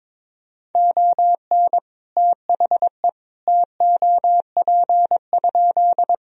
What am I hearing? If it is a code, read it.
Morse code, heard as a keyed tone, about 21 wpm, 700 Hz. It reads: ON THE TOP?
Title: Sentences from Top 1000 Words - Review Part 2 - 22wpm